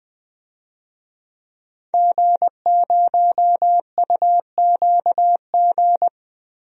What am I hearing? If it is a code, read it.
Morse code, heard as a keyed tone, about 20 wpm, 700 Hz. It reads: G0UQG
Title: DXCC-100 Call Signs - 5 Characters - Repeated 2x - 20wpm